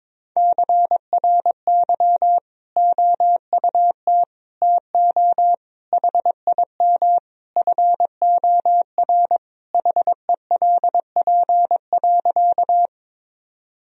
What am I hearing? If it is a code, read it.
Morse code, heard as a keyed tone, about 21 wpm, 700 Hz. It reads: CRY OUT TO HIM FOR HELP.
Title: Sentences from Top 1000 Words - Review Part 2 - 22wpm